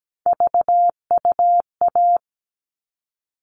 Morse code, heard as a keyed tone, about 17 wpm, 700 Hz. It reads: VUA
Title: All Cut Numbers - 3-Digit 17wpm